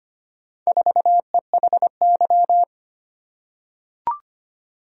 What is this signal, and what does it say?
Morse code, heard as a keyed tone, about 25 wpm, 700 Hz. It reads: 4EHY
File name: practice / 25wpm / Four Letter-Number Combinations - 25wpm